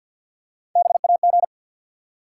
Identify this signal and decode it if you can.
Morse code, heard as a keyed tone, about 53 wpm, 700 Hz. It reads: BAG
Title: Three Letter Words - 50wpm